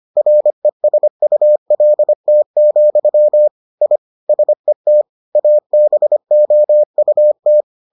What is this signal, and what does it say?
Morse code, heard as a keyed tone, about 25 wpm, 600 Hz. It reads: RESULT, I SET ABOUT